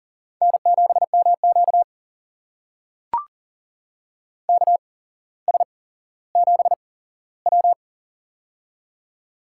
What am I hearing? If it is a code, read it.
Morse code, heard as a keyed tone, about 44 wpm, spaced as for 17 wpm, 700 Hz. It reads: N7MQ KS7W
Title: USA Call Signs - 4 Characters - 8x Character Spacing - 40wpm